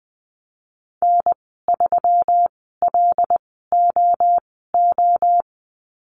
Morse code, heard as a keyed tone, about 20 wpm, spaced as for 14 wpm, 700 Hz. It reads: N3LOO
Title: 128 Lesson 21g - Number 6 - Call Signs 20wpm